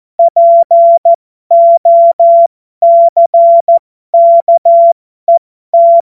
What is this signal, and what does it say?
Morse code, heard as a keyed tone, about 14 wpm, spaced as for 12 wpm, 680 Hz. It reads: POCKET
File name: Morsekurs / Witze